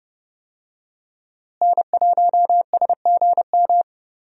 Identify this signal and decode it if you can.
Morse code, heard as a keyed tone, about 30 wpm, 700 Hz. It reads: N1SGM
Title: USA Call Signs - 5 Characters - Repeated 2x - 30wpm